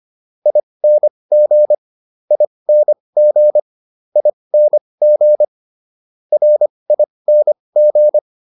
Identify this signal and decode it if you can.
Morse code, heard as a keyed tone, about 25 wpm, spaced as for 20 wpm, 600 Hz. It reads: ING ING ING RING